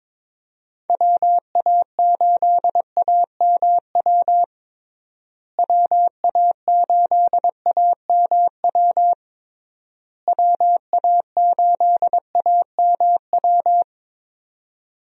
Morse code, heard as a keyed tone, about 21 wpm, 700 Hz. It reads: WA8AMW WA8AMW WA8AMW
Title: USA Call Signs - 6 Characters - Repeated 3x - 22wpm